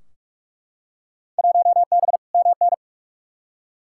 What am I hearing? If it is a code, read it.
Morse code, heard as a keyed tone, about 44 wpm, 700 Hz. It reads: 1B MN
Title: Field Day Exchanges - 45wpm